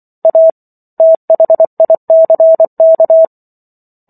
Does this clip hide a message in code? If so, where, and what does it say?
Yes: Morse code from 0.2 to 3.2 s, heard as a keyed tone, about 24 wpm, 650 Hz. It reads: A THICK